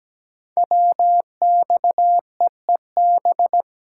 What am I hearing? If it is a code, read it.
Morse code, heard as a keyed tone, about 17 wpm, 700 Hz. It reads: WXEEB
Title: Code Groups - 5 Letters - 17wpm